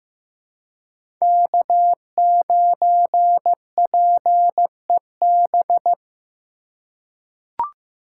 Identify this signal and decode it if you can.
Morse code, heard as a keyed tone, about 15 wpm, 700 Hz. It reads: K9PEB
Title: USA Call Signs - 5 Characters - 6x Character Spacing - 15wpm